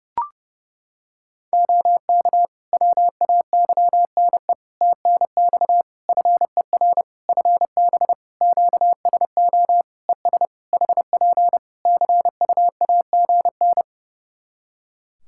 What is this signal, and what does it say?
Morse code, heard as a keyed tone, about 32 wpm, 700 Hz. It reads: OK WAYNE TNX FER FB QSO ES HP CUAGN
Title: Practice QSO - Exchange 4 - Ending - 30wpm